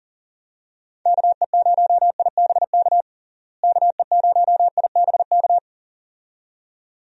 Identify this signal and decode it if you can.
Morse code, heard as a keyed tone, about 37 wpm, 700 Hz. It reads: KE0IBK KE0IBK
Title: USA Call Signs - 6 Characters - Repeated 2x - 40wpm